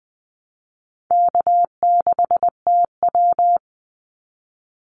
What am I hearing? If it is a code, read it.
Morse code, heard as a keyed tone, about 20 wpm, 700 Hz. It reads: K6TW
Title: NA SOTA Activator Callsigns 20wpm